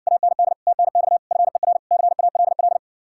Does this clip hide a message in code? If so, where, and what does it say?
Yes: Morse code from 0.1 to 2.8 s, heard as a keyed tone, about 67 wpm, 700 Hz. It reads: AND TNX FER CALL